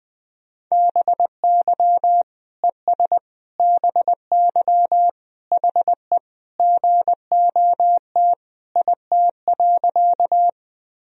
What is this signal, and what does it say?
Morse code, heard as a keyed tone, about 19 wpm, 700 Hz. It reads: BY ES BY HE GOT IT.
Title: Sentences from Top 300 Words and Abbreviations 20wpm